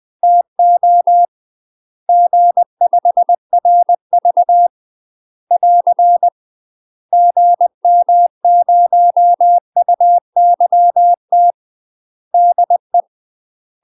Morse code, heard as a keyed tone, about 20 wpm, 700 Hz. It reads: TO G5RV <AR> GM0UYT DE
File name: UK MASTER QSOs TRAINING / 20WPM / UK MASTER QSOs 0066